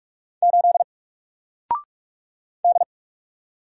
Morse code, heard as a keyed tone, about 46 wpm, 700 Hz. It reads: 8 D
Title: Single Letter-Number 45wpm